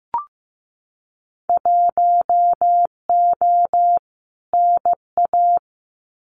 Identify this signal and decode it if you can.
Morse code, heard as a keyed tone, about 15 wpm, 700 Hz. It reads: 1O NA
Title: Winter Field Day Exchanges 15wpm